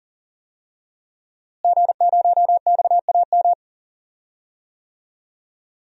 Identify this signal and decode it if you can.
Morse code, heard as a keyed tone, about 37 wpm, 700 Hz. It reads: G0XAM
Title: DXCC-100 Call Signs - 5 Characters - Repeated 3x - 40wpm